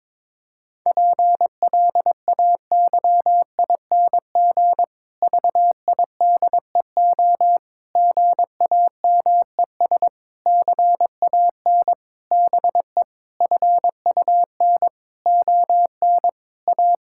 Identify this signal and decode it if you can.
Morse code, heard as a keyed tone, about 22 wpm, 700 Hz. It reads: PLAYING VIDEO GAMES CAN BE FUN ON A
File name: Difficult Sentences with Hints - 22wpm